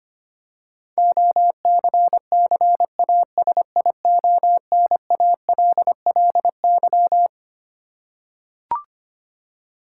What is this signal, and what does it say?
Morse code, heard as a keyed tone, about 25 wpm, 700 Hz. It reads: OCCASIONALLY
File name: Twelve Letter Words - 25wpm